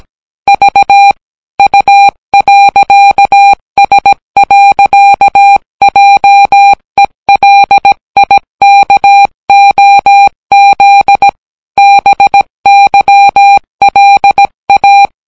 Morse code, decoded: V U.S. JELIKOZ BYLA